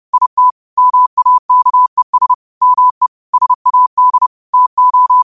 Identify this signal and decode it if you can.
Morse code, heard as a keyed tone, about 30 wpm, 1000 Hz. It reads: IT MAKES ME SAD TO
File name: qotdfiles / 30wpm